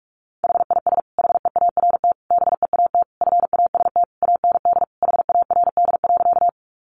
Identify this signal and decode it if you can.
Morse code, heard as a keyed tone, about 44 wpm, 700 Hz. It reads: HIS HEART BEAT FAST AND HARD.
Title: Sentences from Top 1300 Words 45wpm